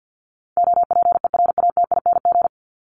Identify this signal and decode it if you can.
Morse code, heard as a keyed tone, about 50 wpm, 700 Hz. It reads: OPERATING
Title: ING Suffix - Top 100 Words with Base Words 50wpm